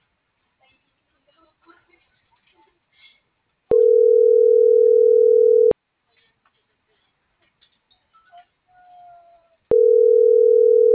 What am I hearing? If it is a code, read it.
Ringback tone (North America), 440 + 480 Hz.